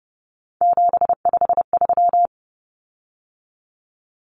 Morse code, heard as a keyed tone, about 30 wpm, 700 Hz. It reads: 753
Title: Common Cut Numbers - 3-Digit 30wpm